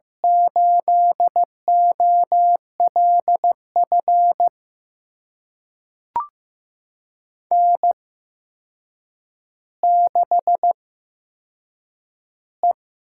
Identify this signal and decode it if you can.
Morse code, heard as a keyed tone, about 15 wpm, spaced as for 7 wpm, 700 Hz. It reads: 8OLF N6E
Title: USA Call Signs - 8x Character Spacing - 15wpm